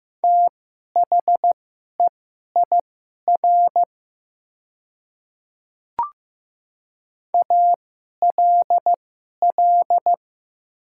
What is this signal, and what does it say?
Morse code, heard as a keyed tone, about 15 wpm, spaced as for 11 wpm, 700 Hz. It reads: THEIR ALL
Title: 2x Character Spacing - Top 100 Words - 15wpm